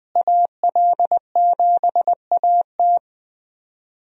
Morse code, heard as a keyed tone, about 20 wpm, 700 Hz. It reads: AL7AT
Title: USA Call Signs - 5 Characters - Repeated 3x - 20wpm